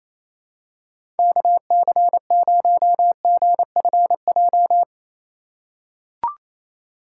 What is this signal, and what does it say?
Morse code, heard as a keyed tone, about 28 wpm, 700 Hz. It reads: KC0GFJ
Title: USA Call Signs - 6 Characters - 2x Character Spacing - 28wpm